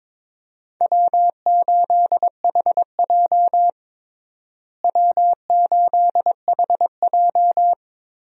Morse code, heard as a keyed tone, about 22 wpm, 700 Hz. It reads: W8HJ W8HJ
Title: USA Call Signs - 4 Characters - Repeated 2x - 22wpm